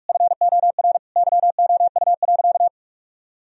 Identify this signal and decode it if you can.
Morse code, heard as a keyed tone, about 46 wpm, 700 Hz. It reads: FOR YOU.
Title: Sentences from Top 1000 Words - Review Part 3 - 45wpm